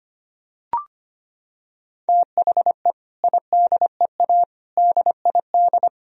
Morse code, heard as a keyed tone, about 25 wpm, 700 Hz. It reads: THE IDEA DID
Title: Sentences from Top 1000 Words - Review Part 1 - 25wpm